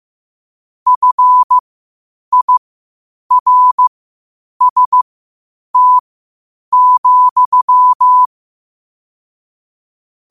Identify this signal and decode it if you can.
Morse code, heard as a keyed tone, about 15 wpm, spaced as for 8 wpm, 1000 Hz. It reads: FIRST,
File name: qotdfiles / 8wpm